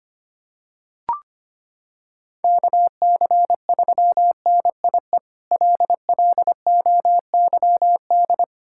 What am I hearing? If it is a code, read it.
Morse code, heard as a keyed tone, about 25 wpm, 700 Hz. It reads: KC3NIE LLOYD
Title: K1USN SST QSO Practice - 25wpm